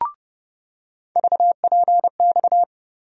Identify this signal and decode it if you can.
Morse code, heard as a keyed tone, about 30 wpm, 700 Hz. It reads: VPX
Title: USA Call Sign Postfixes - 3 Letters 30wpm